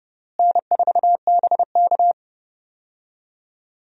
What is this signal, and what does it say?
Morse code, heard as a keyed tone, about 29 wpm, 700 Hz. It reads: N4BK
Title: USA Call Signs - 4 Characters - Repeated 2x - 30wpm